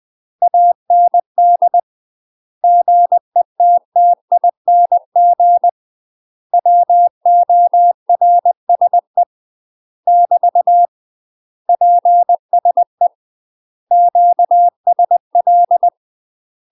Morse code, heard as a keyed tone, about 20 wpm, 700 Hz. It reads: AND GETTING WORSE <BT> PSE QSL
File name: UK MASTER QSOs TRAINING / 20WPM / UK MASTER QSOs 0041